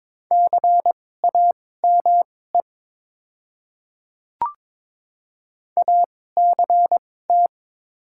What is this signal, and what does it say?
Morse code, heard as a keyed tone, about 22 wpm, spaced as for 16 wpm, 700 Hz. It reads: CAME ACT
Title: 2x Character Spacing - Top 200 Words - 22wpm